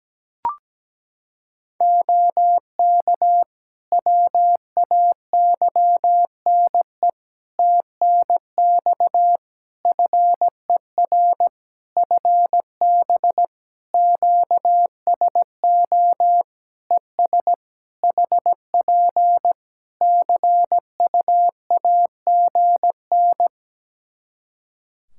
Morse code, heard as a keyed tone, about 17 wpm, 700 Hz. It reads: OK WAYNE TNX FER FB QSO ES HP CUAGN
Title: Practice QSO - Exchange 4 - Ending - 17wpm